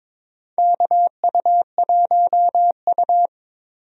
Morse code, heard as a keyed tone, about 22 wpm, 700 Hz. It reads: KU1U
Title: USA Call Signs - 4 Characters - Repeated 2x - 22wpm